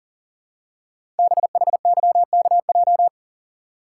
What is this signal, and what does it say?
Morse code, heard as a keyed tone, about 39 wpm, 700 Hz. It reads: BHYKJ
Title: Code Groups - 5 Letters - 40wpm